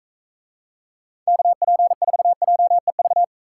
Morse code, heard as a keyed tone, about 45 wpm, 700 Hz. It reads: KP4JEV